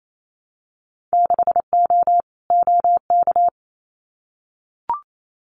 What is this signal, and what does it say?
Morse code, heard as a keyed tone, about 28 wpm, 700 Hz. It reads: 6O OK
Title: Winter Field Day Exchanges Encore 28wpm